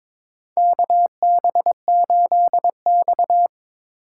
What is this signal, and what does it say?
Morse code, heard as a keyed tone, about 22 wpm, 700 Hz. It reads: KB8X
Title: USA Call Signs - 4 Characters - 4x Character Spacing - 22wpm